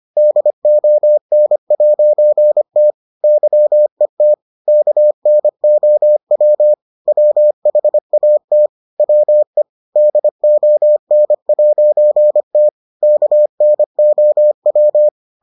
Morse code, heard as a keyed tone, about 25 wpm, 600 Hz. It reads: DON'T YET KNOW WHAT WE DON'T KNOW